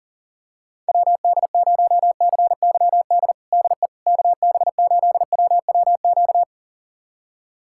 Morse code, heard as a keyed tone, about 40 wpm, 700 Hz. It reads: WD0CYD DE KB8WWQ
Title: USA Callsign DE Callsign - 6 Characters - 40wpm